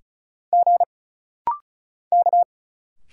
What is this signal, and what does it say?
Morse code, heard as a keyed tone, about 36 wpm, 700 Hz. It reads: G K
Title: Single Letters - Rapid-Fire 35wpm